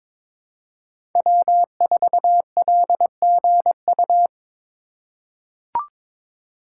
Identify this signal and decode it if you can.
Morse code, heard as a keyed tone, about 22 wpm, 700 Hz. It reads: W4LGU
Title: Code Groups - 5 Letter:Numbers - 22wpm